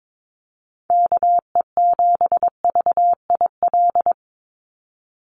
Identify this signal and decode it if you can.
Morse code, heard as a keyed tone, about 22 wpm, 700 Hz. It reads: KE7VIL
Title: USA Call Signs 22wpm